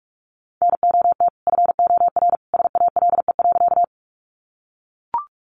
Morse code, heard as a keyed tone, about 44 wpm, 700 Hz. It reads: NOT FOR SALE.
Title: Sentences from Top 1400 Words 45wpm